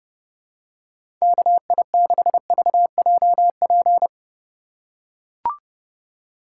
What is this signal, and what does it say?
Morse code, heard as a keyed tone, about 30 wpm, 700 Hz. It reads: KI6VJP
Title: USA Call Signs - 6 Characters - 4x Character Spacing - 30wpm